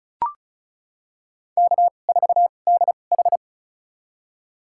Morse code, heard as a keyed tone, about 36 wpm, spaced as for 25 wpm, 700 Hz. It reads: K4DH
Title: USA Call Signs - 4 Characters - 2x Character Spacing - 35wpm